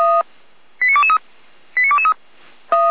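Five-tone selective call (ZVEI), heard as quick runs of five tones, tones of about 70 ms. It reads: 89223 89223
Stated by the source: Five-tone ZVEI selective call received on SDR.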